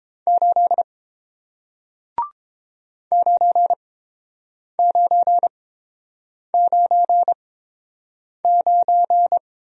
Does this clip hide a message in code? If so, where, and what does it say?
Yes: Morse code from 0.3 to 9.4 s, heard as a keyed tone, about 30 wpm, spaced as for 7 wpm, 700 Hz. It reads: 8 9999